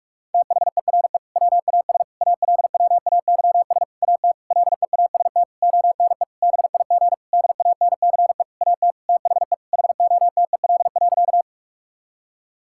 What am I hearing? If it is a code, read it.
Morse code, heard as a keyed tone, about 46 wpm, 700 Hz. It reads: THERE WAS ALWAYS AT LEAST ONE BIG DANCE AT THE HOTEL.
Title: Sentences from Top 1600 Words 45wpm